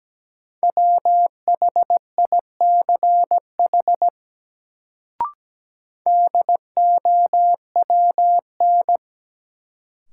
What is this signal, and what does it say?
Morse code, heard as a keyed tone, about 17 wpm, 700 Hz. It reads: WHICH DOWN
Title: Sets of 2 Words - New Word Drill - 17wpm